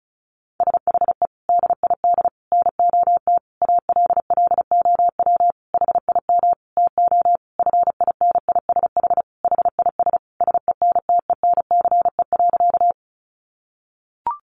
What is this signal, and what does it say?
Morse code, heard as a keyed tone, about 34 wpm, 700 Hz. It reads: SHE DID NOT ALLOW HIM TO FINISH HIS SENTENCE.
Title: Sentences from Top 1200 Words 35wpm